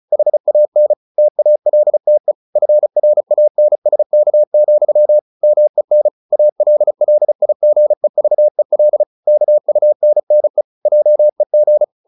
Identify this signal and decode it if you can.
Morse code, heard as a keyed tone, about 34 wpm, 600 Hz. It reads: HAN TALTE FRANSK, MEN ALLIGEVEL KUNNE JEG
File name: Ferie QRQ 34wpm